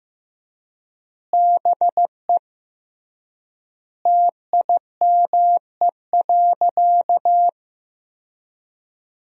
Morse code, heard as a keyed tone, about 15 wpm, 700 Hz. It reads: BE TIME.
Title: 3x Word-Spacing - Sentences from Top 200 Words - 15wpm